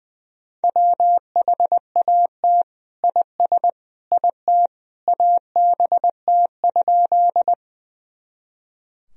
Morse code, heard as a keyed tone, about 20 wpm, 700 Hz. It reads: WHAT IS IT ABT?
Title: Sentences from Top 100 Words and Abbreviations 20wpm